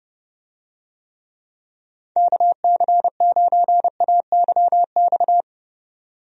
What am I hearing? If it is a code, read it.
Morse code, heard as a keyed tone, about 30 wpm, 700 Hz. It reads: KC9AYX
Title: USA Call Signs - 6 Characters - 30wpm